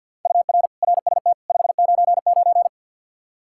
Morse code, heard as a keyed tone, about 48 wpm, 700 Hz. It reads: UR RST 599
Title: Practice QSO - Exchange 1 - Beginning - 50wpm